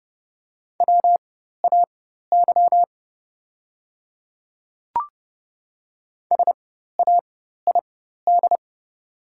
Morse code, heard as a keyed tone, about 31 wpm, spaced as for 14 wpm, 690 Hz. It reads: WAY SAID
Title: 4x Character Spacing - Top 200 Words - 30wpm